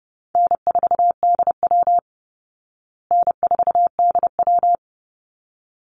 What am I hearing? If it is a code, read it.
Morse code, heard as a keyed tone, about 28 wpm, 700 Hz. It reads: N4DW N4DW
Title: Call Signs Repeated 3x - 30wpm